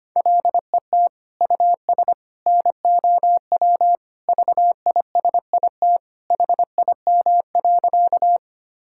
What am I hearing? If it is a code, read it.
Morse code, heard as a keyed tone, about 25 wpm, 700 Hz. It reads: LET US NOW VISIT HIM.
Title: Sentences from Top 1000 Words - Review Part 6 - 25wpm